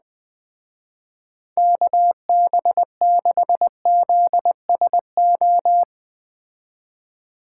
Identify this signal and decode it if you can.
Morse code, heard as a keyed tone, about 20 wpm, 700 Hz. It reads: KB6ZSO
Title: Speed Racing - USA 2x3 Call Signs 17wpm